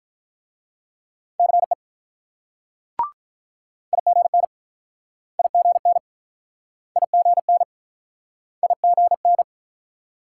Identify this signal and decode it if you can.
Morse code, heard as a keyed tone, about 47 wpm, 700 Hz. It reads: CE IGN IGN IGN IGN